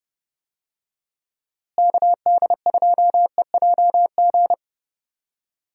Morse code, heard as a keyed tone, about 31 wpm, 700 Hz. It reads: KD2EJG
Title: USA Call Signs - 6 Characters - Repeated 3x - 30wpm